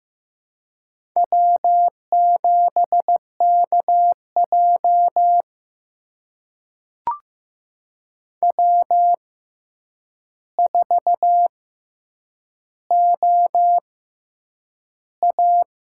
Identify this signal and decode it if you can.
Morse code, heard as a keyed tone, about 15 wpm, spaced as for 8 wpm, 700 Hz. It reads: W7KJ W4OA
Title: USA Call Signs - 4 Characters - 6x Character Spacing - 15wpm